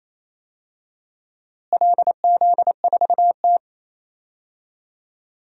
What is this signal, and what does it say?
Morse code, heard as a keyed tone, about 28 wpm, 700 Hz. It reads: LZ4T
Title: DXCC-100 Call Signs - 4 Characters - Repeated 2x - 28wpm